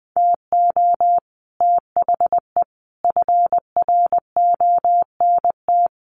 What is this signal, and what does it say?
Morse code, heard as a keyed tone, about 20 wpm, 700 Hz. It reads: TO THE FRONT
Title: Sets of 5 Words 20wpm